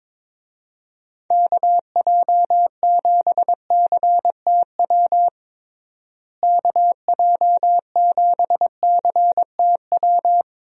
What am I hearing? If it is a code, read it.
Morse code, heard as a keyed tone, about 22 wpm, 700 Hz. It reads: KJ7CTW KJ7CTW
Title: USA Call Signs - 6 Characters - Repeated 3x - 22wpm